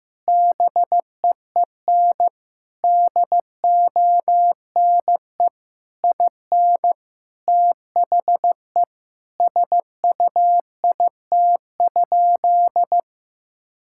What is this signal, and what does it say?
Morse code, heard as a keyed tone, about 15 wpm, 700 Hz. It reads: BEEN DONE IN THE SUIT?
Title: Sentences from Top 1600 Words 15wpm